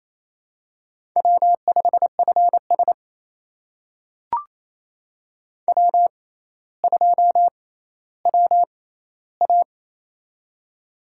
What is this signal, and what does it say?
Morse code, heard as a keyed tone, about 27 wpm, spaced as for 14 wpm, 700 Hz. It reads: W5FS W2WA
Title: USA Call Signs - 4 Characters - 6x Character Spacing - 28wpm